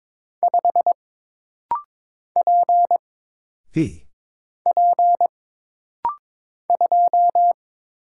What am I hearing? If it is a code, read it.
Morse code, heard as a keyed tone, about 22 wpm, 700 Hz. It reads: 5 P P 2
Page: This rapid-fire timing is geared to push your ICR (Instant Character Recognition) skills.